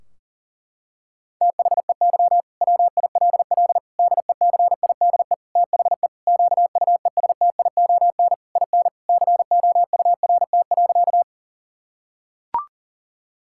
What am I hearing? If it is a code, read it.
Morse code, heard as a keyed tone, about 40 wpm, 700 Hz. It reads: THEY WILL DECIDE THE QUESTION IN COURT.
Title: Sentences from Top 1600 Words 40wpm